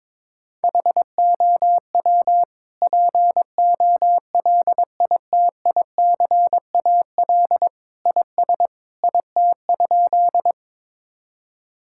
Morse code, heard as a keyed tone, about 22 wpm, 700 Hz. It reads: HOW POLITICAL IS IT?
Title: Sentences from Top 1000 Words - Review Part 5 - 22wpm